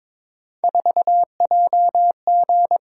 Morse code, heard as a keyed tone, about 22 wpm, 700 Hz. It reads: 4JG
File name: practice / 22wpm / Three Letter-Number Combinations - 22wpm